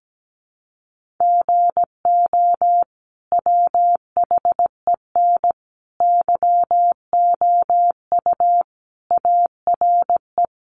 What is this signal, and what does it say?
Morse code, heard as a keyed tone, about 17 wpm, 700 Hz. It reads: GO WHEN YOU ARE
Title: Sentences from Top 500 Words 17wpm